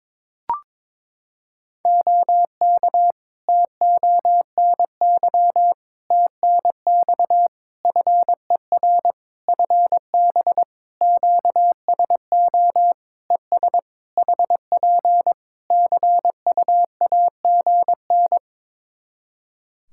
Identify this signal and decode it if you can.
Morse code, heard as a keyed tone, about 22 wpm, 700 Hz. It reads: OK TONY TNX FER FB QSO ES HP CUAGN
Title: Practice QSO - Exchange 4 - Ending - 22wpm